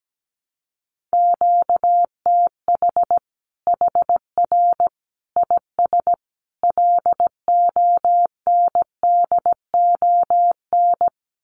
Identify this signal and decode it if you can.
Morse code, heard as a keyed tone, about 17 wpm, 700 Hz. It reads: QTH HR IS LONDON
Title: Common QSO Phrases 17wpm